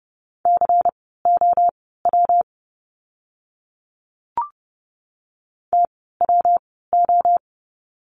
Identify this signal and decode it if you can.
Morse code, heard as a keyed tone, about 30 wpm, spaced as for 17 wpm, 700 Hz. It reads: COW TWO
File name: course / 30:10wpm / 115 Lesson 19f - Letter w - Words 30wpm